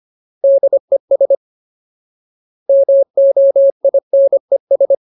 Morse code, heard as a keyed tone, about 25 wpm, 550 Hz. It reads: DES MOINES
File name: Ultimate Word List / 25WPM Ultimate